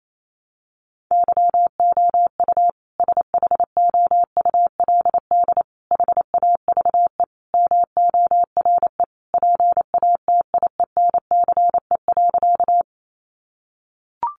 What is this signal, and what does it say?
Morse code, heard as a keyed tone, about 28 wpm, 700 Hz. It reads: YOU SHOULD HAVE MORE PATIENCE.
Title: Sentences from Top 1300 Words 28wpm